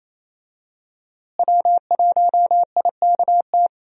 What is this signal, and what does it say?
Morse code, heard as a keyed tone, about 28 wpm, 700 Hz. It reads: W1IKT
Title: USA Call Signs - 5 Characters - Repeated 2x - 28wpm